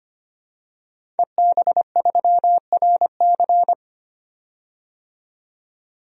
Morse code, heard as a keyed tone, about 25 wpm, 700 Hz. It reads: EB3RC
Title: DXCC-100 Call Signs - 5 Characters - 25wpm